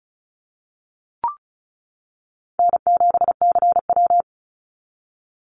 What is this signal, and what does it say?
Morse code, heard as a keyed tone, about 36 wpm, 700 Hz. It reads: N7CW
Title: NA SOTA Activator Callsigns 35wpm